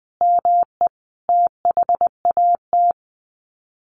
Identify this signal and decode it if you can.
Morse code, heard as a keyed tone, about 20 wpm, 700 Hz. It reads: ME THAT
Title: Sets of 5 Words Encore 20wpm